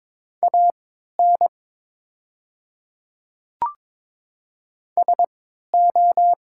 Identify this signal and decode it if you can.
Morse code, heard as a keyed tone, about 22 wpm, spaced as for 13 wpm, 700 Hz. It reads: AN SO
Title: 3x Character Spacing - Top 200 Words - 22wpm